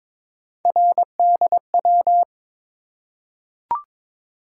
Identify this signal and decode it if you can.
Morse code, heard as a keyed tone, about 22 wpm, 700 Hz. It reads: RDW
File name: practice / 22wpm / Three Letter-Number Combinations - 22wpm